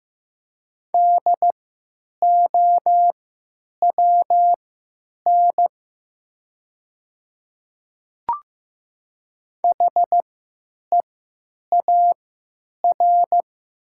Morse code, heard as a keyed tone, about 15 wpm, spaced as for 9 wpm, 700 Hz. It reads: DOWN HEAR
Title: 3x Character Spacing - Top 500 Words - 15wpm